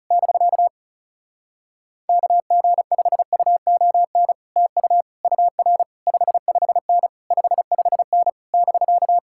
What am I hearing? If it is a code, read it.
Morse code, heard as a keyed tone, about 40 wpm, 700 Hz. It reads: <BK> KG5UON TU UR 55N 55N <BK>